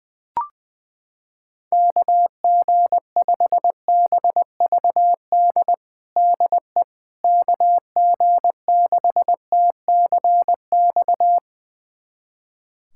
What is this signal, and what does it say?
Morse code, heard as a keyed tone, about 20 wpm, 700 Hz. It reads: KG5BVD DE KG6TCX
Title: USA Callsign DE Callsign 20wpm